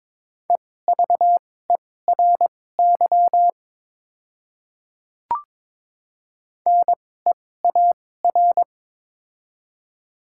Morse code, heard as a keyed tone, about 22 wpm, spaced as for 16 wpm, 700 Hz. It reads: EVERY NEAR
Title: 2x Character Spacing - Top 200 Words - 22wpm